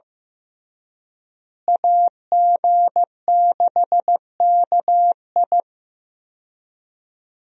Morse code, heard as a keyed tone, about 16 wpm, 700 Hz. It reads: AG6KI
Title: USA Call Signs - 5 Characters - Repeated 3x - 15wpm